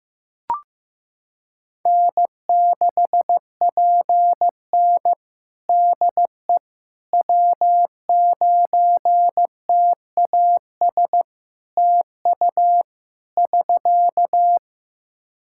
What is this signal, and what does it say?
Morse code, heard as a keyed tone, about 15 wpm, 700 Hz. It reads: N6PN DE W9TAS TU <SK>
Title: Practice QSO - Exchange 4 - Ending - 15wpm